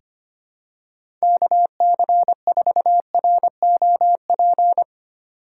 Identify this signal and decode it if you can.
Morse code, heard as a keyed tone, about 25 wpm, 700 Hz. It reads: KC4ROP
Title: USA Call Signs - 6 Characters - 3x Character Spacing - 25wpm